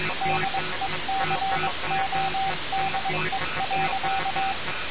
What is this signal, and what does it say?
Morse code, heard as a keyed tone, about 22 wpm, 770 Hz, with noise. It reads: DECODE.
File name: radiocap / REC z